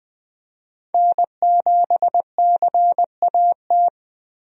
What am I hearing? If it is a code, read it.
Morse code, heard as a keyed tone, about 21 wpm, 700 Hz. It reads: N7CAT
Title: USA Call Signs - 5 Characters - Repeated 3x - 20wpm